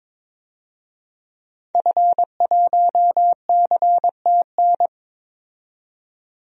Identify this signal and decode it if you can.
Morse code, heard as a keyed tone, about 21 wpm, 700 Hz. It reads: F1CTN
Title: DXCC-100 Call Signs - 5 Characters - 22wpm